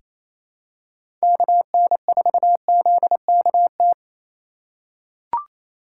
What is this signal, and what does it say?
Morse code, heard as a keyed tone, about 28 wpm, 700 Hz. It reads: KN4ZKT
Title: K1USN SST QSO Practice - 28wpm